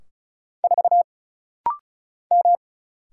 Morse code, heard as a keyed tone, about 33 wpm, 700 Hz. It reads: 4 M
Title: Single Letter-Number - Rapid-Fire 35wpm